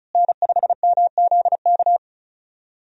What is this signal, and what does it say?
Morse code, heard as a keyed tone, about 36 wpm, 700 Hz. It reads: N5MZK
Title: USA Call Signs - 5 Characters - Repeated 2x - 35wpm